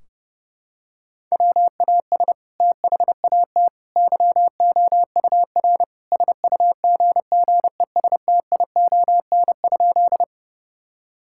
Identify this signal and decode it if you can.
Morse code, heard as a keyed tone, about 32 wpm, 700 Hz. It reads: WAS THAT YOUR SUGGESTION?
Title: Sentences from Top 1600 Words 30wpm